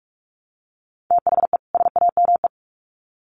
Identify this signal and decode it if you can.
Morse code, heard as a keyed tone, about 44 wpm, 700 Hz. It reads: THE SAME
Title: Sets of 2 Words 45wpm